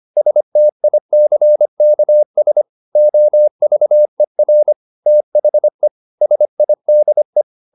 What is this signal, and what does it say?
Morse code, heard as a keyed tone, about 25 wpm, 600 Hz. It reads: STICKS OVER THE SIDE
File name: the old man and the sea part 1 25wpm